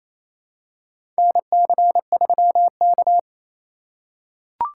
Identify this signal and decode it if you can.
Morse code, heard as a keyed tone, about 28 wpm, 700 Hz. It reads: NC3K
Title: USA Call Signs - 4 Characters - 3x Character Spacing - 28wpm